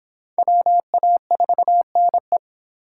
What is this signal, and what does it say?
Morse code, heard as a keyed tone, about 22 wpm, 700 Hz. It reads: WA4NE